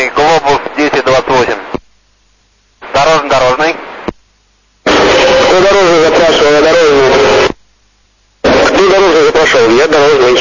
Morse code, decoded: none